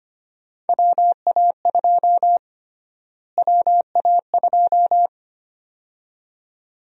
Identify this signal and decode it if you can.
Morse code, heard as a keyed tone, about 25 wpm, 700 Hz. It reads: WA2 WA2
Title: USA Call Sign Prefixes - Repeated 2x - 25wpm